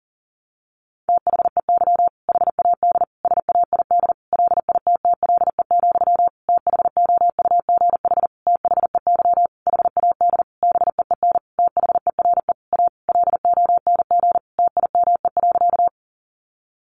Morse code, heard as a keyed tone, about 40 wpm, 700 Hz. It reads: THEY HAD SAID LITTLE, THOUGH THEY HAD BEEN THERE A LONG TIME.
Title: Sentences from Top 400 Words 40wpm